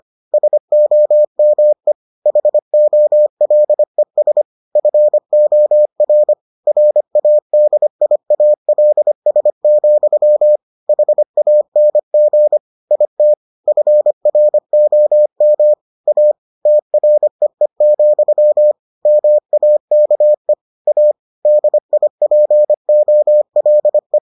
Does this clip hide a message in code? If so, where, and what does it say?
Yes: Morse code from 0.3 to 24.2 s, heard as a keyed tone, about 25 wpm, 600 Hz. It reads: SOME HOLES FOR RADIALS, HANG IT FROM A TREE, MAKE A DIPOLE